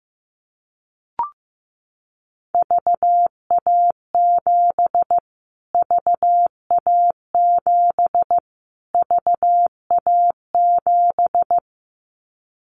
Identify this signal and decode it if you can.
Morse code, heard as a keyed tone, about 15 wpm, 700 Hz. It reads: VA7 VA7 VA7
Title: Callsign Prefixes for Top 25 DXCC - Repeated 3x - 15wpm